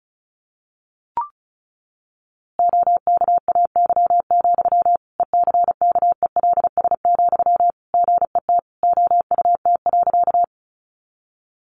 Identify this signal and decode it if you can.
Morse code, heard as a keyed tone, about 34 wpm, 700 Hz. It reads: OKAY, ECKELS, GET OUT.
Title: A Sound Of Thunder 35wpm